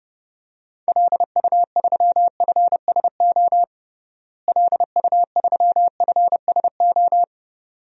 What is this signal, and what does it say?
Morse code, heard as a keyed tone, about 32 wpm, 700 Hz. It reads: LU3FSO LU3FSO
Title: DXCC-100 Call Signs - 6 Characters - Repeated 2x - 30wpm